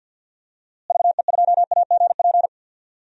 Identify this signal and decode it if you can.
Morse code, heard as a keyed tone, about 53 wpm, 700 Hz. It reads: VE2AGP
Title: DXCC-100 Call Signs - 6 Characters - Repeated 2x - 50wpm